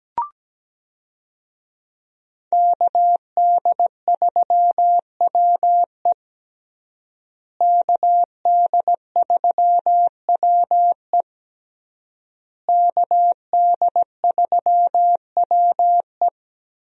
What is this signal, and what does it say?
Morse code, heard as a keyed tone, about 17 wpm, 700 Hz. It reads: KD3WE KD3WE KD3WE
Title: USA Call Signs - 5 Characters - Repeated 3x - 17wpm